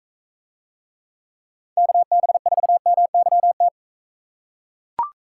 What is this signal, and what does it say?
Morse code, heard as a keyed tone, about 40 wpm, 700 Hz. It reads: KB4MYT